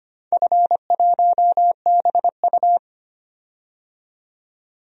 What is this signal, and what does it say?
Morse code, heard as a keyed tone, about 25 wpm, 700 Hz. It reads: F1BU
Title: DXCC-100 Call Signs - 4 Characters - 25wpm